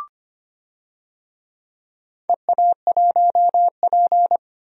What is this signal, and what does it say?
Morse code, heard as a keyed tone, about 25 wpm, 690 Hz. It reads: EA1P
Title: DXCC-100 Call Signs - 4 Characters - 25wpm